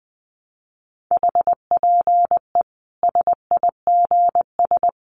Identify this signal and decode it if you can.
Morse code, heard as a keyed tone, about 20 wpm, 700 Hz. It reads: HPE SIGS
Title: Common QSO Phrases 20wpm